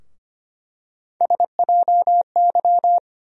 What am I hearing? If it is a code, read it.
Morse code, heard as a keyed tone, about 25 wpm, 700 Hz. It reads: SJY
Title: USA Call Sign Postfixes - 3 Letters 25wpm